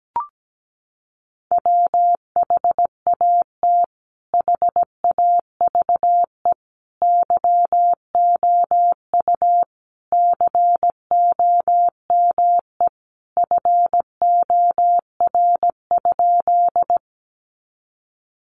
Morse code, heard as a keyed tone, about 17 wpm, 700 Hz. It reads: WHAT HAVE YOU COME FOR?
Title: Sentences from Top 100 Words 17wpm